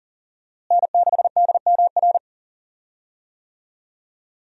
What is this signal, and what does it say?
Morse code, heard as a keyed tone, about 40 wpm, 700 Hz. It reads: N6DMR
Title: USA Call Signs - 5 Characters - Repeated 3x - 40wpm